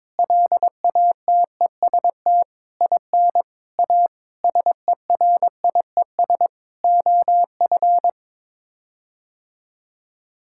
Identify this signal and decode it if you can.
Morse code, heard as a keyed tone, about 22 wpm, 700 Hz. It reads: LATEST IN A SERIES OF
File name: practice / 22wpm / Sets of 8 Words - 22wpm